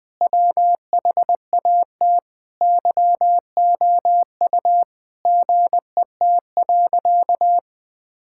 Morse code, heard as a keyed tone, about 20 wpm, 700 Hz. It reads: WHAT YOU GET.
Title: Sentences with 7-Word Phrases - 20wpm